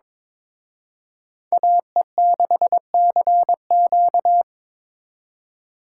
Morse code, heard as a keyed tone, about 25 wpm, 700 Hz. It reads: AE6CQ
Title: Speed Racing - USA 2x2 Call Signs 22wpm